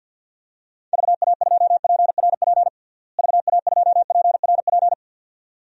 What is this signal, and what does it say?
Morse code, heard as a keyed tone, about 48 wpm, 700 Hz. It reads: VA2PRP VA2PRP
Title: DXCC-100 Call Signs - 6 Characters - Repeated 2x - 50wpm